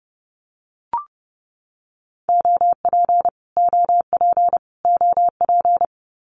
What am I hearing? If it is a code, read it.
Morse code, heard as a keyed tone, about 30 wpm, 700 Hz. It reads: OP OP OP
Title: Abbreviations Repeated 3x - 30wpm